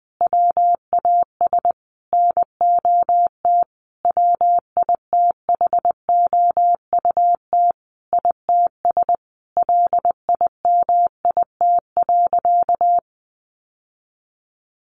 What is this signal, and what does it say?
Morse code, heard as a keyed tone, about 20 wpm, 700 Hz. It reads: WAS NOT WITHOUT ITS LIMIT.
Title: Sentences from Top 1500 Words 20wpm